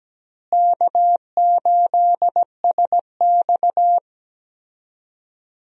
Morse code, heard as a keyed tone, about 17 wpm, 700 Hz. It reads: K8SX
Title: USA Call Signs - 4 Characters - Repeated 3x - 17wpm